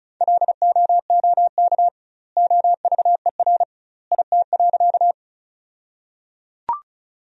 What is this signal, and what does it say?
Morse code, heard as a keyed tone, about 36 wpm, 700 Hz. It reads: LOOK OVER IT.
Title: 2x Word-Spacing - Sentences from Top 100 Words - 35wpm